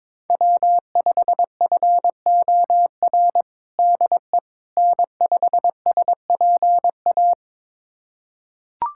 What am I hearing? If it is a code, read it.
Morse code, heard as a keyed tone, about 22 wpm, 700 Hz. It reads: W5FOR DE N5SPA
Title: USA Callsign DE Callsign - 5 Characters - 22wpm